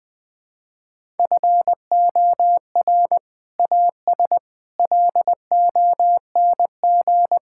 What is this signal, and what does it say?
Morse code, heard as a keyed tone, about 21 wpm, 700 Hz. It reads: FOR AS LONG
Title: Sets of 8 Words - 20wpm